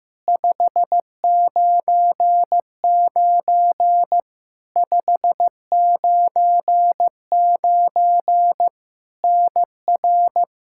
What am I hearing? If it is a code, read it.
Morse code, heard as a keyed tone, about 16 wpm, 700 Hz. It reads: 599 599 NR
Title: K1USN SST QSO Practice 15wpm